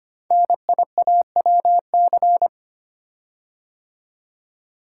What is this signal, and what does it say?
Morse code, heard as a keyed tone, about 25 wpm, 700 Hz. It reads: NIAWC
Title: Code Groups - 5 Letters - 25wpm